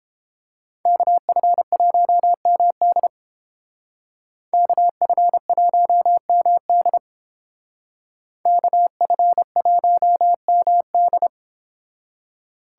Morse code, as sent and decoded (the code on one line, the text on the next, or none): -.- ..-. .---- -- -.. / -.- ..-. .---- -- -.. / -.- ..-. .---- -- -..
KF1MD KF1MD KF1MD